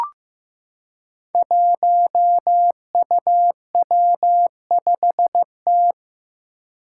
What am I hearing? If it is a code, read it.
Morse code, heard as a keyed tone, about 15 wpm, 700 Hz. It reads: 1UW5T
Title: Five Challenging Letter-Number Combinations 15wpm